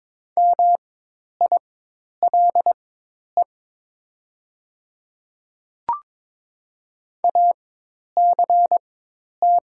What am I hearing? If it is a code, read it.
Morse code, heard as a keyed tone, about 22 wpm, spaced as for 10 wpm, 700 Hz. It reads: MILE ACT
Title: 4x Character Spacing - Top 300 Words - 22wpm